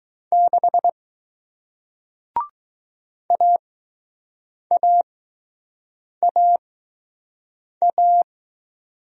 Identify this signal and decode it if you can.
Morse code, heard as a keyed tone, about 20 wpm, spaced as for 6 wpm, 700 Hz. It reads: 6 AAAA